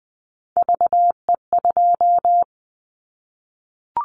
Morse code, heard as a keyed tone, about 20 wpm, 700 Hz. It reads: VE2
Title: Callsign Prefixes for Top 25 DXCC - 20wpm